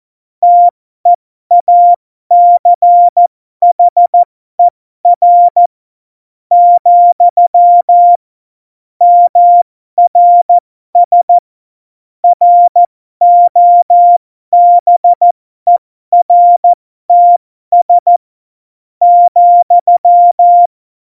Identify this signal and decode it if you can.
Morse code, heard as a keyed tone, about 14 wpm, spaced as for 12 wpm, 700 Hz. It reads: TEACHER , MRS ROBERTS ,
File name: Morsekurs / Witze